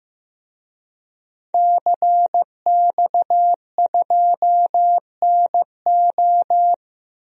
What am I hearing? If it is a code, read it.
Morse code, heard as a keyed tone, about 15 wpm, 700 Hz. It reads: CX2NO
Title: DXCC-100 Call Signs - 5 Characters - 15wpm